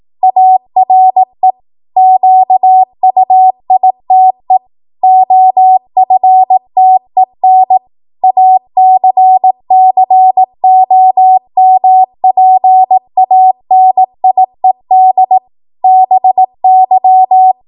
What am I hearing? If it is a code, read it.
Morse code, heard as a keyed tone, about 18 wpm, 750 Hz. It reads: ARE QUITE OFTEN ACCOMPANIED BY
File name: Morse / 18WPM